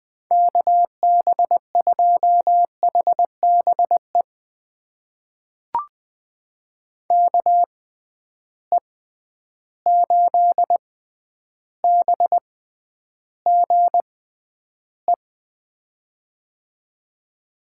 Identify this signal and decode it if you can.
Morse code, heard as a keyed tone, about 20 wpm, spaced as for 10 wpm, 700 Hz. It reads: KB2HBE KE8BGE
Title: USA Call Signs - 6 Characters - 6x Character Spacing - 20wpm